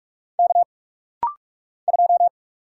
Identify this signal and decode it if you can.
Morse code, heard as a keyed tone, about 44 wpm, 700 Hz. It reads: K 2
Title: Single Letter-Number - Rapid-Fire 45wpm